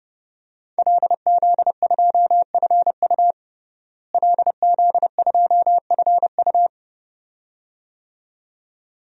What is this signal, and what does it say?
Morse code, heard as a keyed tone, about 32 wpm, 700 Hz. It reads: LZ2FU LZ2FU
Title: DXCC-100 Call Signs - 5 Characters - Repeated 2x - 30wpm